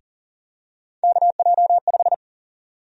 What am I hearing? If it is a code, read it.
Morse code, heard as a keyed tone, about 40 wpm, 700 Hz. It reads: KJ5
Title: USA Call Sign Prefixes - Repeated 2x - 40wpm